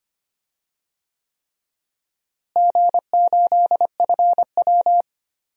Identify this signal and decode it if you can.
Morse code, heard as a keyed tone, about 25 wpm, 700 Hz. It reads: G8FW
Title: DXCC-100 Call Signs - 4 Characters - Repeated 3x - 25wpm